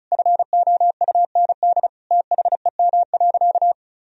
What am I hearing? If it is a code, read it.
Morse code, heard as a keyed tone, about 34 wpm, 700 Hz. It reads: FOUND THEM.
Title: Sentences from Top 1000 Words - Review Part 1 - 35wpm